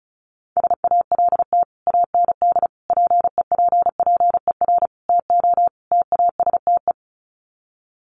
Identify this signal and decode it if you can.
Morse code, heard as a keyed tone, about 34 wpm, 700 Hz. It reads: SALT AND PEPPER TO TASTE
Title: Sets of 5 Words 35wpm